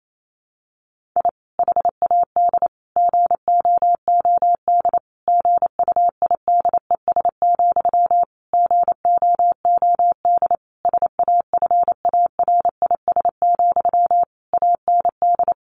Morse code, heard as a keyed tone, about 28 wpm, 700 Hz. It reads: I HAD GOOD GUIDES, GOOD SAFARIS, AND
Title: A Sound Of Thunder 28wpm